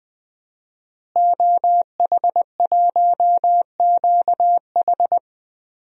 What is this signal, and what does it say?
Morse code, heard as a keyed tone, about 20 wpm, 700 Hz. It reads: OH1QH
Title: DXCC-100 Call Signs - 5 Characters - Repeated 3x - 20wpm